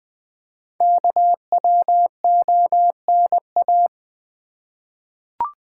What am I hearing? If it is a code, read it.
Morse code, heard as a keyed tone, about 20 wpm, 700 Hz. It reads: KWONA
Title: Code Groups - 5 Letters - 20wpm